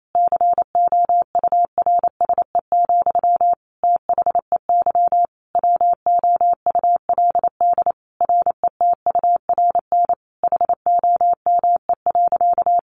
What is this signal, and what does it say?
Morse code, heard as a keyed tone, about 28 wpm, 700 Hz. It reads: COURSE, THEY WOULD RETURN HOME.
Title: Sentences from Top 1400 Words 28wpm